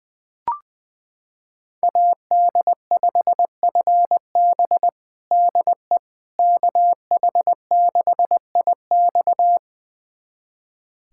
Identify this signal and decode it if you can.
Morse code, heard as a keyed tone, about 21 wpm, 700 Hz. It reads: AD5FB DE KH6IX
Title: USA Callsign DE Callsign - 5 Characters - 20wpm